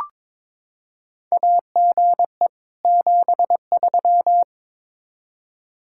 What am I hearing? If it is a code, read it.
Morse code, heard as a keyed tone, about 22 wpm, 700 Hz. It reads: AGE 73
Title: K1USN SST QSO Practice 22wpm